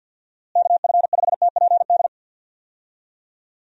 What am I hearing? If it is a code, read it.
Morse code, heard as a keyed tone, about 52 wpm, 700 Hz. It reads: KF5TPD
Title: USA Call Signs - 6 Characters - Repeated 2x - 50wpm